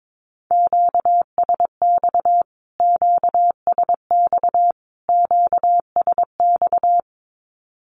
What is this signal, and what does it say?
Morse code, heard as a keyed tone, about 22 wpm, 700 Hz. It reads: QSX QSX QSX
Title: Q-Codes Repeated 3x 22wpm